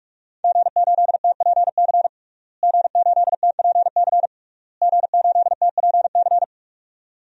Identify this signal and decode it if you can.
Morse code, heard as a keyed tone, about 42 wpm, 700 Hz. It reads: G8TPC G8TPC G8TPC
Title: DXCC-100 Call Signs - 5 Characters - Repeated 3x - 45wpm